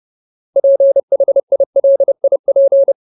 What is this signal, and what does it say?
Morse code, heard as a keyed tone, about 30 wpm, 550 Hz. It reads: PHILIP
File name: Ultimate Word List / 30WPM Ultimate